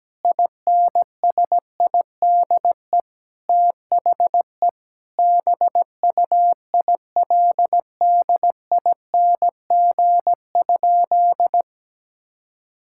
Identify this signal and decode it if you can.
Morse code, heard as a keyed tone, about 17 wpm, 700 Hz. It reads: INSIDE THE BUILDING?
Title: Sentences from Top 1000 Words - Review Part 6 - 17wpm